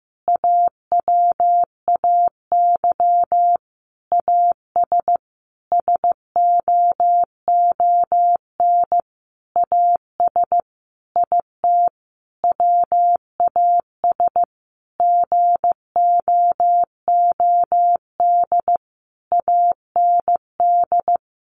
Morse code, heard as a keyed tone, about 15 wpm, 700 Hz. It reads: AWAY AS SOON AS IT WAS GOOD AND
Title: Sentences from Top 600 Words 15wpm